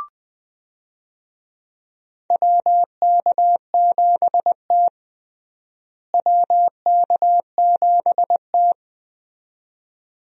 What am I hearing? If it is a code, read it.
Morse code, heard as a keyed tone, about 20 wpm, 700 Hz. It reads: WK7T WK7T
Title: USA Call Signs - 4 Characters - Repeated 2x - 20wpm